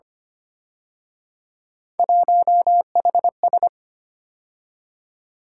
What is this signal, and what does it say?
Morse code, heard as a keyed tone, about 29 wpm, 700 Hz. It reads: 1HS
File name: Speed-Racing - Three Letter-Number Combinations - 25wpm